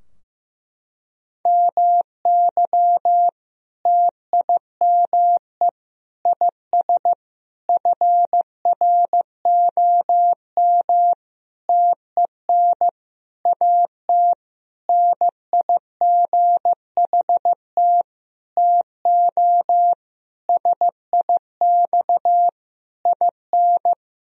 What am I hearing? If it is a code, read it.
Morse code, heard as a keyed tone, about 15 wpm, 700 Hz. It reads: MY TIME IS FROM TEN AT NIGHT TO SIX IN
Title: Sentences from Top 1000 Words - Review Part 2 - 15wpm